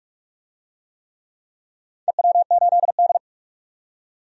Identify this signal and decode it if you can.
Morse code, heard as a keyed tone, about 47 wpm, 700 Hz. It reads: EW8D
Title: DXCC-100 Call Signs - 4 Characters - 45wpm